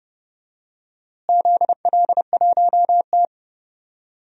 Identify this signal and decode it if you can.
Morse code, heard as a keyed tone, about 30 wpm, 700 Hz. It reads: ZL1T
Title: DXCC-100 Call Signs - 4 Characters - Repeated 2x - 30wpm